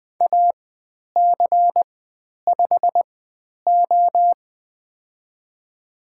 Morse code, heard as a keyed tone, about 20 wpm, spaced as for 10 wpm, 700 Hz. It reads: AC5O
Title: 109 Lesson 18g - Letter m - Call Signs 20wpm